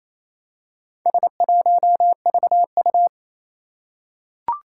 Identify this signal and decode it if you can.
Morse code, heard as a keyed tone, about 27 wpm, 700 Hz. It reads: S1VU
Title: Four Challenging Letter-Number Combinations 28wpm